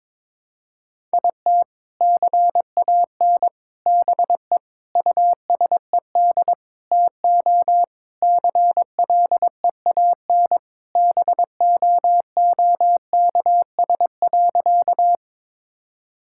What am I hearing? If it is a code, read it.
Morse code, heard as a keyed tone, about 22 wpm, 700 Hz. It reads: IT CAN BE USED TO CLEAN BOOKS.
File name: practice / 22wpm / Sentences with 4-Word Phrases - 22wpm